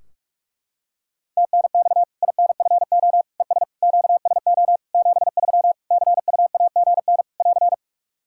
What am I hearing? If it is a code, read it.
Morse code, heard as a keyed tone, about 46 wpm, 700 Hz. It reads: TNX INFO ES QSO 73 CUAGN <AR>
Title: Practice QSO - Exchange 4 - Ending - 45wpm